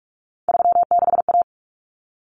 Morse code, heard as a keyed tone, about 44 wpm, 700 Hz. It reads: 36A
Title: Common Cut Numbers - 3-Digit 45wpm